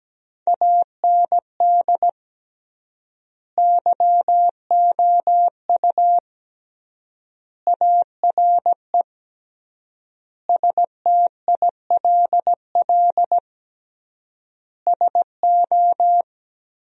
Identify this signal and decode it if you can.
Morse code, heard as a keyed tone, about 17 wpm, 700 Hz. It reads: AND YOU ARE STILL SO
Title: 3x Word-Spacing - Sentences from Top 400 Words - 17wpm